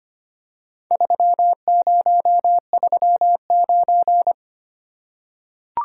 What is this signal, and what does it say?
Morse code, heard as a keyed tone, about 25 wpm, 700 Hz. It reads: 3039
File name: practice / 25wpm / Four Letter-Number Combinations - 25wpm